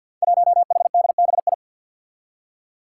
Morse code, heard as a keyed tone, about 46 wpm, 700 Hz. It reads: 1HDBI
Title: Five Challenging Letter-Number Combinations 50wpm